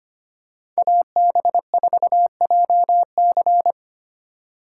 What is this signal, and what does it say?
Morse code, heard as a keyed tone, about 25 wpm, 700 Hz. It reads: AB4JC
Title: USA Call Signs - 5 Characters - 4x Character Spacing - 25wpm